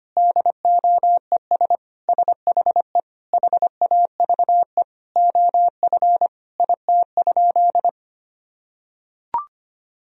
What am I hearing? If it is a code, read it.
Morse code, heard as a keyed tone, about 25 wpm, 700 Hz. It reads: DOES SHE HAVE OF IT?
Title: Sentences from Top 1000 Words - Review Part 3 - 25wpm